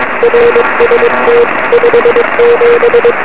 Morse code, heard as a keyed tone, about 22 wpm, 470 Hz, with radio noise. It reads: RST 57
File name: 2111z pa2m 23cm